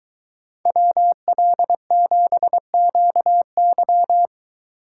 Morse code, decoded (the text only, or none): WL7QY